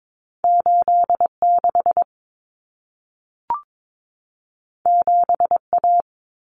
Morse code, heard as a keyed tone, about 22 wpm, 700 Hz. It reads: 86 7A
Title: Common Cut Numbers - 2-Digit 22wpm